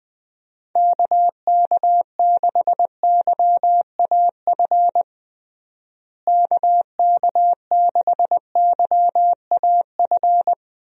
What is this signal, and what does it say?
Morse code, heard as a keyed tone, about 19 wpm, 700 Hz. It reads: KK6YAF KK6YAF
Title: USA Call Signs - 6 Characters - Repeated 3x - 20wpm